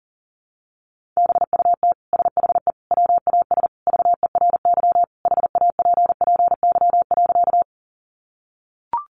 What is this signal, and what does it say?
Morse code, heard as a keyed tone, about 40 wpm, 700 Hz. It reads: BUT SHE WAS VERY HAPPY.
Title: Sentences from Top 1000 Words 40wpm